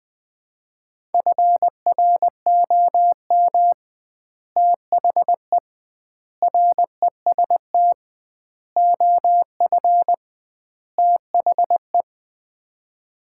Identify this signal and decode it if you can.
Morse code, heard as a keyed tone, about 20 wpm, 700 Hz. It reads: FROM THE REST OF THE
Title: 2x Word-Spacing - Sets of 5 Words - 20wpm